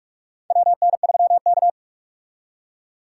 Morse code, heard as a keyed tone, about 46 wpm, 700 Hz. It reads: WN3K
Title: K1USN SST QSO Practice - 45wpm